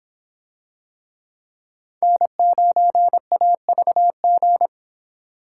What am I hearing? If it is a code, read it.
Morse code, heard as a keyed tone, about 25 wpm, 700 Hz. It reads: N9AVG